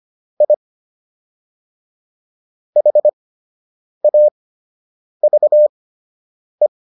Morse code, heard as a keyed tone, about 25 wpm, spaced as for 8 wpm, 600 Hz. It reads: I HAVE